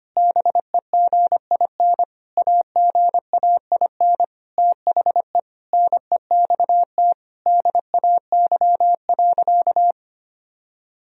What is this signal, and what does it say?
Morse code, heard as a keyed tone, about 25 wpm, 700 Hz. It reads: BEGIN AGAIN THE NEXT DAY.
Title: Sentences from Top 1000 Words - Review Part 1 - 25wpm